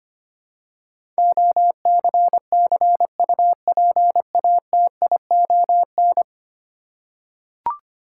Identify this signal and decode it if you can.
Morse code, heard as a keyed tone, about 25 wpm, 700 Hz. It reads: OCCUPATION
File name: 10 Letter Words - First 3 Letters Given - 25wpm